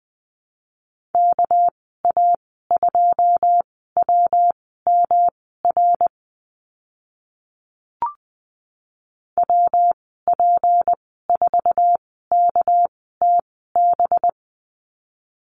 Morse code, heard as a keyed tone, about 20 wpm, spaced as for 15 wpm, 700 Hz. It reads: KA2WMR WP4KTB
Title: 245 Lesson 39g - Letter z - Call Signs 20wpm